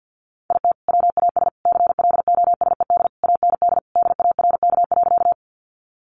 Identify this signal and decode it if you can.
Morse code, heard as a keyed tone, about 50 wpm, 700 Hz. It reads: IT WAS CLOSED AND DARK.
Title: Sentences from Top 1400 Words 50wpm